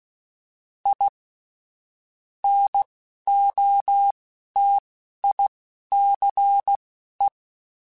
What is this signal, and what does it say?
Morse code, heard as a keyed tone, about 16 wpm, spaced as for 11 wpm, 800 Hz. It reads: I NOTICE